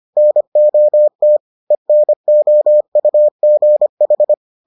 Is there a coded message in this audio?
Yes: Morse code from 0.2 to 4.3 s, heard as a keyed tone, about 25 wpm, 600 Hz. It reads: NOT ENOUGH